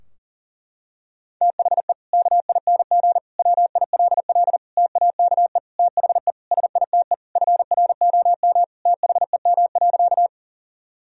Morse code, heard as a keyed tone, about 40 wpm, 700 Hz. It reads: THE KING WILL TAKE THE SITE FROM THEM.
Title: Sentences from Top 1000 Words - Review Part 4 - 40wpm